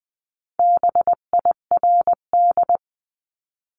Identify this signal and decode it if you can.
Morse code, heard as a keyed tone, about 20 wpm, 700 Hz. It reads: BIRD
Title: 259 Lesson 42d - Words 18-20wpm